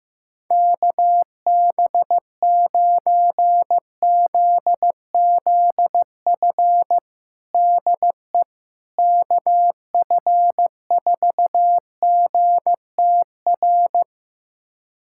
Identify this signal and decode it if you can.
Morse code, heard as a keyed tone, about 15 wpm, 700 Hz. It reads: KB9ZZF DE KF4GTR
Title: USA Callsign DE Callsign - 6 Characters - 15wpm